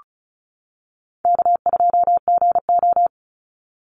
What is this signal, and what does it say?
Morse code, heard as a keyed tone, about 34 wpm, 700 Hz. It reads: K2GO
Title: Call Signs - 35wpm